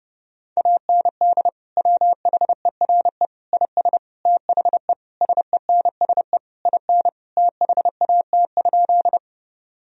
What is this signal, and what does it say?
Morse code, heard as a keyed tone, about 30 wpm, 700 Hz. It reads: AND WHERE IS THE SENSE IN THAT?
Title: Sentences from Top 1000 Words - Review Part 4 - 30wpm